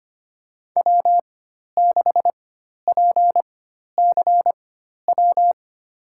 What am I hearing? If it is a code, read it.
Morse code, heard as a keyed tone, about 25 wpm, spaced as for 12 wpm, 700 Hz. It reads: W6PCW
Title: USA Call Signs - 4x Character Spacing - 25wpm